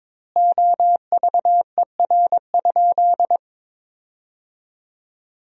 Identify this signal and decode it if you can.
Morse code, heard as a keyed tone, about 22 wpm, 700 Hz. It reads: OVER?
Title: 4x Word-Spacing - Sentences from Top 100 Words - 22wpm